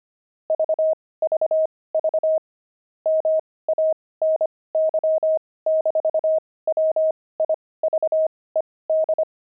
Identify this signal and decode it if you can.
Morse code, heard as a keyed tone, about 25 wpm, spaced as for 18 wpm, 630 Hz. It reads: VVV MANY-WIVED
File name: pangram-fs-18-cs-25-wived